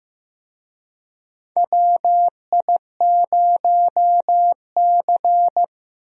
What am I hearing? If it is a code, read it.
Morse code, heard as a keyed tone, about 15 wpm, 700 Hz. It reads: WI0C
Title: USA Call Signs - 4 Characters - 15wpm